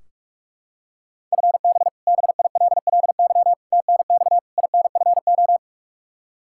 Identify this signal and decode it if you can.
Morse code, heard as a keyed tone, about 42 wpm, 700 Hz. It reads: FB BILLY TNX INFO
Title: K1USN SST QSO Practice 45wpm